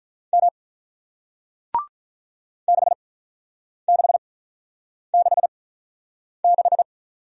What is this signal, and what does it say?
Morse code, heard as a keyed tone, about 49 wpm, spaced as for 9 wpm, 700 Hz. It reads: M 6666